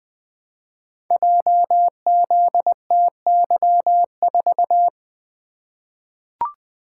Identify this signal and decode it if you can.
Morse code, heard as a keyed tone, about 20 wpm, 700 Hz. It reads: JZTY4
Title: Code Groups - 5 Letter:Numbers - 20wpm